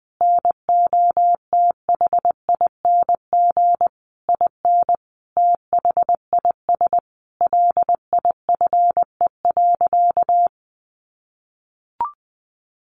Morse code, heard as a keyed tone, about 19 wpm, 700 Hz. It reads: NOTHING IN THIS LIFE.
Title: Sentences from Top 1200 Words 20wpm